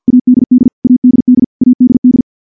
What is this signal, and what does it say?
Morse code, heard as a keyed tone, about 50 wpm, 250 Hz. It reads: ADD ADD ADD
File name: morse / morse-training-50WPM